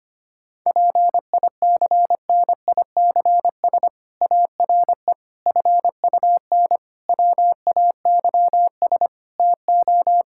Morse code, heard as a keyed tone, about 25 wpm, 700 Hz. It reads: PICNICS ARE FUN WAYS TO
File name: practice / 25wpm / Difficult Sentences with Hints - 25wpm